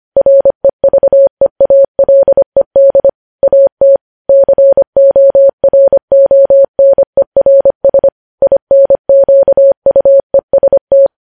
Morse code, heard as a keyed tone, about 25 wpm, 550 Hz. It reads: REVEALED AT CORONERS INQUEST
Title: News Headlines at 25